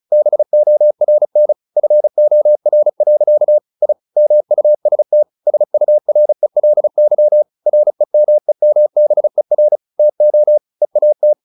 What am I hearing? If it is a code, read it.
Morse code, heard as a keyed tone, about 34 wpm, 600 Hz. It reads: BORN FOR. I MUST SURELY REMEMBER TO EAT
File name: the old man and the sea / the old man and the sea 1 35wpm